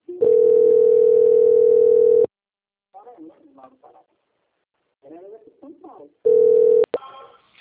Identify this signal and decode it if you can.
Ringback tone (North America), 440 + 480 Hz, with speech.
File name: out- - - - - .9694